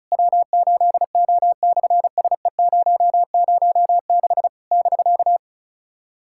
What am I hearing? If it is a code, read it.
Morse code, heard as a keyed tone, about 36 wpm, 700 Hz. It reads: W8O/SE006 <BK>
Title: SOTA QSO Practice 35wpm